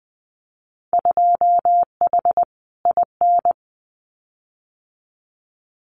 Morse code, heard as a keyed tone, about 20 wpm, 700 Hz. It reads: 2H IN
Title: Winter Field Day Exchanges Encore 20wpm